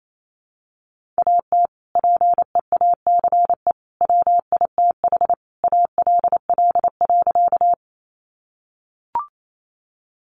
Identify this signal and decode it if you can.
Morse code, heard as a keyed tone, about 28 wpm, 700 Hz. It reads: AT PEACE WITH ALL.
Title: Sentences from Top 1500 Words 28wpm